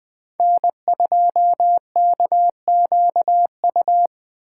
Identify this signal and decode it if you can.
Morse code, heard as a keyed tone, about 20 wpm, 700 Hz. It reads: N2KQU
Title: Speed Racing - USA 1x3 Call Signs 17wpm